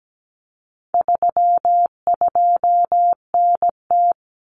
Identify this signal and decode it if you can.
Morse code, heard as a keyed tone, about 17 wpm, 700 Hz. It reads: 32NT
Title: Common Cut Numbers - 4-Digit 17wpm